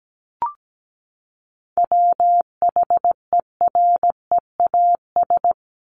Morse code, heard as a keyed tone, about 17 wpm, 700 Hz. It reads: WHEREAS
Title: Top 1501-1750 Words 17wpm